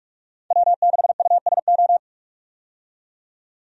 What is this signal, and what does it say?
Morse code, heard as a keyed tone, about 41 wpm, 700 Hz. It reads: W6USO
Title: USA Call Signs - 5 Characters - 45wpm